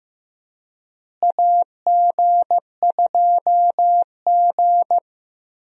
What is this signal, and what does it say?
Morse code, heard as a keyed tone, about 15 wpm, 700 Hz. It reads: AG2G
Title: USA Call Signs - 4 Characters - Repeated 3x - 15wpm